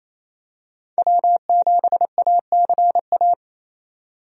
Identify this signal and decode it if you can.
Morse code, heard as a keyed tone, about 27 wpm, 700 Hz. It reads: W7ACA
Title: USA Call Signs - 5 Characters - 28wpm